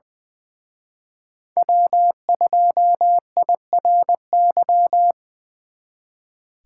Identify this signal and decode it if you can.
Morse code, heard as a keyed tone, about 24 wpm, 700 Hz. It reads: W2IRY
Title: Speed Racing - USA 1x3 Call Signs 20wpm